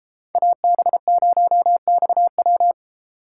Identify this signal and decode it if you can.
Morse code, heard as a keyed tone, about 33 wpm, 700 Hz. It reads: AB0XW